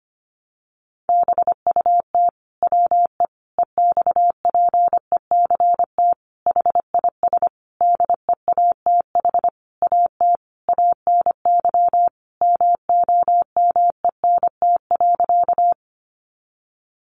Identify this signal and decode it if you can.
Morse code, heard as a keyed tone, about 25 wpm, 700 Hz. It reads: BUT WE EXPECT HIS DEATH AT ANY MOMENT.
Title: Sentences from Top 1100 Words 25wpm